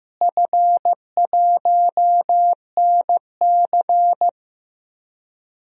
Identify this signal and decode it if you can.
Morse code, heard as a keyed tone, about 15 wpm, 700 Hz. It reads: F1NC
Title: DXCC-100 Call Signs - 4 Characters - Repeated 2x - 15wpm